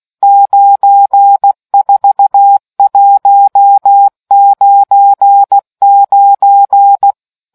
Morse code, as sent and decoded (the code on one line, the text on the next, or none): ----. ....- .---- ----. ----.
94199